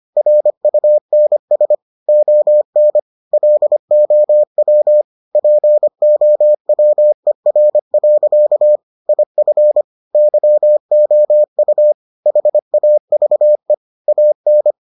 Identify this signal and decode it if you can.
Morse code, heard as a keyed tone, about 25 wpm, 600 Hz. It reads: RUNS ON LOW POWER. IF YOU HAVE AN